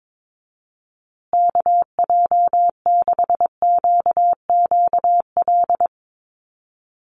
Morse code, heard as a keyed tone, about 22 wpm, 700 Hz. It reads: KJ6QQL
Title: USA Call Signs Repeated 3x 22wpm